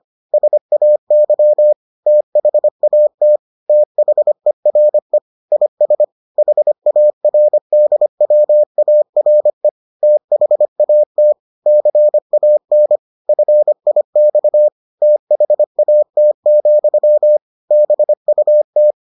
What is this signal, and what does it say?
Morse code, heard as a keyed tone, about 25 wpm, 600 Hz. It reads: SAY THAT THERE IS HARDWARE THAT CAN FIX THAT, BUT